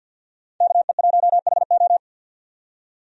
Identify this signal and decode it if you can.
Morse code, heard as a keyed tone, about 50 wpm, 700 Hz. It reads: KE1HO
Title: USA Call Signs - 5 Characters - 50wpm